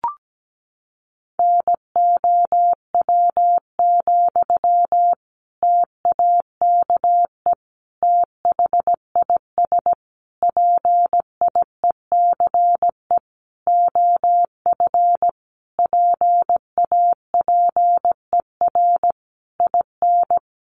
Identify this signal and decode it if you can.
Morse code, heard as a keyed tone, about 17 wpm, 700 Hz. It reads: NOW, TAKE THIS PIECE OF PAPER IN
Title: Sentences from Top 400 Words 17wpm